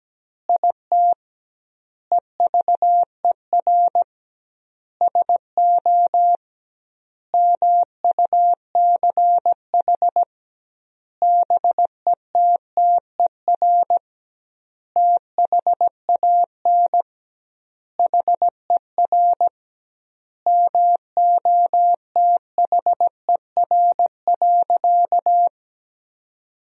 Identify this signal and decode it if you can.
Morse code, heard as a keyed tone, about 17 wpm, 700 Hz. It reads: IT EVER SO MUCH BETTER THAN HER MOTHER.
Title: 2x Word-Spacing - Sentences from Top 400 Words - 17wpm